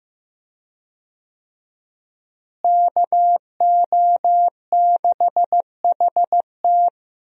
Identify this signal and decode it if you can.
Morse code, heard as a keyed tone, about 15 wpm, 700 Hz. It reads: KO6HT
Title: USA Call Signs - 5 Characters - Repeated 2x - 15wpm